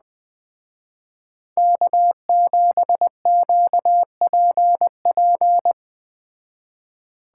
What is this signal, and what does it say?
Morse code, heard as a keyed tone, about 21 wpm, 700 Hz. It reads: K7QPP